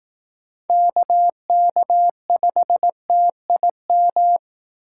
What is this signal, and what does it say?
Morse code, heard as a keyed tone, about 18 wpm, 700 Hz. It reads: KK5TIM
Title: Speed Racing - USA 2x3 Call Signs 15wpm